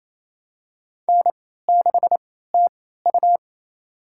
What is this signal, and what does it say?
Morse code, heard as a keyed tone, about 27 wpm, spaced as for 16 wpm, 700 Hz. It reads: N6TU
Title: USA Call Signs - 4 Characters - 3x Character Spacing - 28wpm